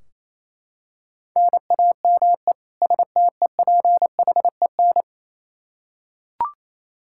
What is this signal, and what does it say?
Morse code, heard as a keyed tone, about 28 wpm, 700 Hz. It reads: NAME STEPHEN
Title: Practice QSO - Exchange 1 - Beginning - 28wpm